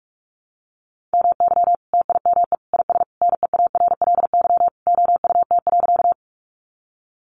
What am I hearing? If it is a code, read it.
Morse code, heard as a keyed tone, about 44 wpm, 700 Hz. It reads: MY TIME IS NEARLY OUT.
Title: Sentences from Top 1300 Words 45wpm